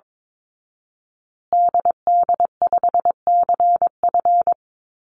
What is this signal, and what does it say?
Morse code, heard as a keyed tone, about 22 wpm, 700 Hz. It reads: DD5CF
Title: Call Signs Repeated 2x - 22wpm